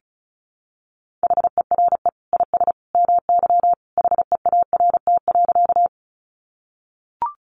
Morse code, heard as a keyed tone, about 34 wpm, 700 Hz. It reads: HERE IS MY HEART.
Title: Sentences from Top 1200 Words 35wpm